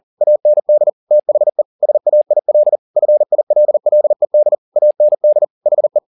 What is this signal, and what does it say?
Morse code, heard as a keyed tone, about 40 wpm, 600 Hz. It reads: AND THE SAIL FILLED AND HE
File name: the old man and the sea 2 40wpm